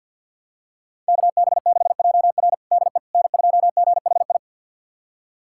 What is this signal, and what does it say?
Morse code, heard as a keyed tone, about 52 wpm, 700 Hz. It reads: KB6JR DE N2GHI
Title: USA Callsign DE Callsign 50wpm